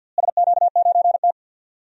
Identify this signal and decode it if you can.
Morse code, heard as a keyed tone, about 48 wpm, 700 Hz. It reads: SQ9T
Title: DXCC-100 Call Signs - 4 Characters - Repeated 2x - 50wpm